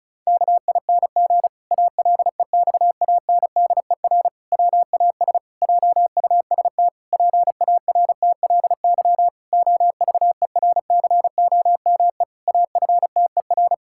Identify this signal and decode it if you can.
Morse code, heard as a keyed tone, about 36 wpm, 700 Hz. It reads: KING ALEXANDER WAS JUST PARTLY OVERCOME AFTER
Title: Pangram Challenge v2 - 35wpm